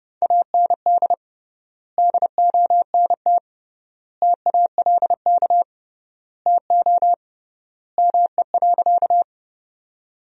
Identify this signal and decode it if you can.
Morse code, heard as a keyed tone, about 32 wpm, 700 Hz. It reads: AND DONT TALK TO ME.
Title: 3x Word-Spacing - Sentences from Top 400 Words - 30wpm